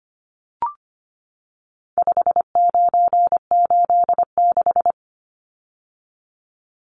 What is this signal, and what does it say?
Morse code, heard as a keyed tone, about 25 wpm, 700 Hz. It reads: 5986
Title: Numbers - 4-Digit 25wpm